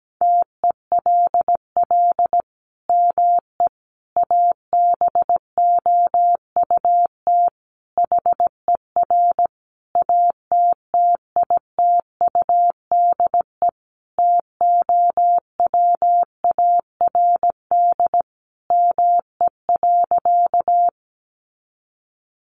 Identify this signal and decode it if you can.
Morse code, heard as a keyed tone, about 17 wpm, 700 Hz. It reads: TELL ME ABOUT HER ATTITUDE TOWARD ME.
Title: Sentences from Top 1300 Words 17wpm